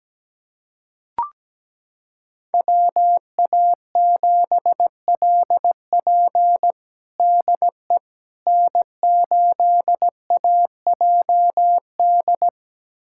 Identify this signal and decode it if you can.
Morse code, heard as a keyed tone, about 17 wpm, 700 Hz. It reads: WA7LP DE N8AJD
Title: USA Callsign DE Callsign - 5 Characters - 17wpm